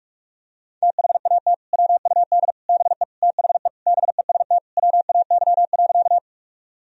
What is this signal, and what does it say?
Morse code, heard as a keyed tone, about 41 wpm, 700 Hz. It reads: THAT WUD BE THE BEST WAY.
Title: Sentences from Top 400 Words and Abbreviations 45wpm